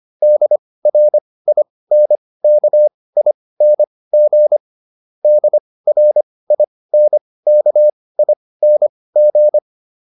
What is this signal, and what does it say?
Morse code, heard as a keyed tone, about 25 wpm, spaced as for 18 wpm, 600 Hz. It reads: DRINKING DRINKING